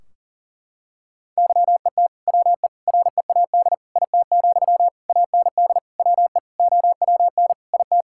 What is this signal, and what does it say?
Morse code, heard as a keyed tone, about 40 wpm, 700 Hz. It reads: YET WE READ IT, AND WE OWN IT
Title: Sentences from Top 1600 Words 40wpm